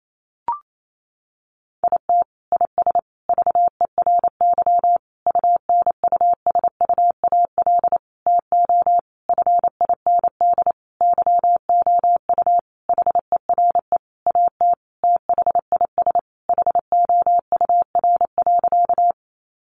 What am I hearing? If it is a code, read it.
Morse code, heard as a keyed tone, about 28 wpm, 700 Hz. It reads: IT IS VERY UNUSUAL TO FIND YOU HERE AT THIS HOUR.
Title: Sentences from Top 1400 Words 28wpm